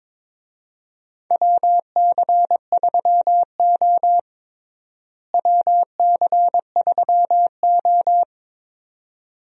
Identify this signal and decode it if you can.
Morse code, heard as a keyed tone, about 22 wpm, 700 Hz. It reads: WC3O WC3O
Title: USA Call Signs - 4 Characters - Repeated 2x - 22wpm